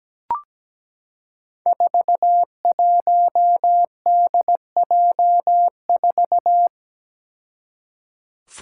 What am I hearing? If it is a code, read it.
Morse code, heard as a keyed tone, about 17 wpm, 700 Hz. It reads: 41DJ4
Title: Five Challenging Letter-Number Combinations 17wpm